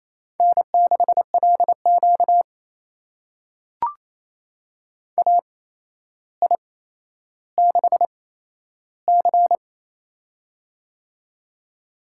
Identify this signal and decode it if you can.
Morse code, heard as a keyed tone, about 27 wpm, spaced as for 12 wpm, 700 Hz. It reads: N6LQ AI6C
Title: USA Call Signs - 4 Characters - 8x Character Spacing - 28wpm